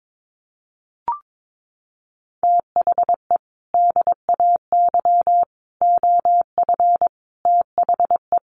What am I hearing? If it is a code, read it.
Morse code, heard as a keyed tone, about 22 wpm, 700 Hz. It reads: THE DAY OF THE
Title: Sentences from Top 1400 Words 22wpm